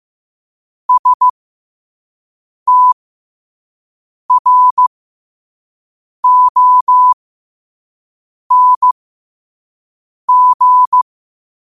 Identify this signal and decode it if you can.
Morse code, heard as a keyed tone, about 15 wpm, spaced as for 5 wpm, 1000 Hz. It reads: STRONG
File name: qotdfiles / 5wpm